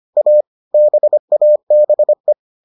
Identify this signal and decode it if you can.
Morse code, heard as a keyed tone, about 25 wpm, 600 Hz. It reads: A BABE